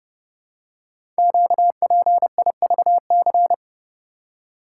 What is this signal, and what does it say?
Morse code, heard as a keyed tone, about 30 wpm, 700 Hz. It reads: QPIVC
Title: Code Groups - 5 Letters - 30wpm